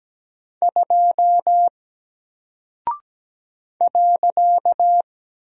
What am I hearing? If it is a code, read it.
Morse code, heard as a keyed tone, about 17 wpm, 700 Hz. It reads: 2 .
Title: Single Character (Letter:Number:Punctuation:Prosign) Alt 17wpm